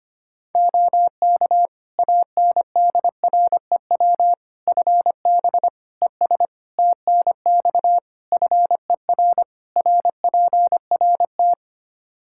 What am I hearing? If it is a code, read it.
Morse code, heard as a keyed tone, about 25 wpm, 700 Hz. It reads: OK ANDREW FB ES TNX FER RPRT
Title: Practice QSO - Exchange 2 - Info - 25wpm